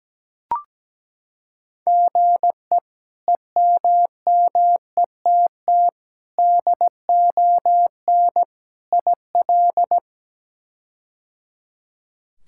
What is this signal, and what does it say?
Morse code, heard as a keyed tone, about 17 wpm, 700 Hz. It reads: GE EMMETT DON IL
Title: K1USN SST QSO Practice - 17wpm